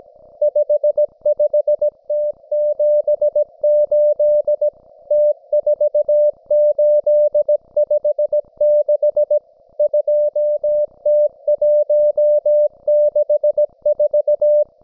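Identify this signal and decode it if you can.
Morse code, heard as a keyed tone, about 17 wpm, 600 Hz. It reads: 55T78 T4856 2T164
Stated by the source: Russian military Morse station, M32, RDL, on 5268 kHz.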